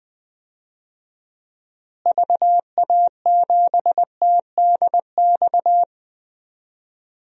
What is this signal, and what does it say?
Morse code, heard as a keyed tone, about 20 wpm, 700 Hz. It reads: VA7TDX
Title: DXCC-100 Call Signs - 6 Characters - Repeated 2x - 20wpm